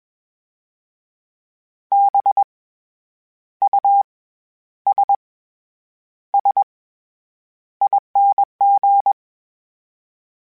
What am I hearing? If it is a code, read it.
Morse code, heard as a keyed tone, about 21 wpm, 800 Hz. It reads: B U S S ING
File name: Special INGs-mp3s / Special INGs-mp3s / SlowToFaster ing @20